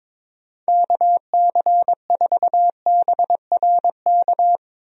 Morse code, heard as a keyed tone, about 22 wpm, 700 Hz. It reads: KC4BRK
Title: USA Call Signs - 6 Characters - Repeated 2x - 22wpm